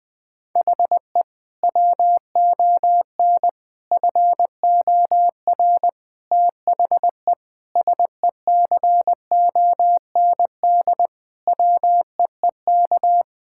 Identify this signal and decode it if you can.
Morse code, heard as a keyed tone, about 20 wpm, 700 Hz. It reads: HE WON FOR THE SECOND WEEK
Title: Sentences with 7-Word Phrases - Spoken Beginning - 20wpm